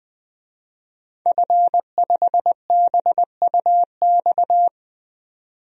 Morse code, heard as a keyed tone, about 20 wpm, 700 Hz. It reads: F5BUX
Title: DXCC-100 Call Signs - 5 Characters - 20wpm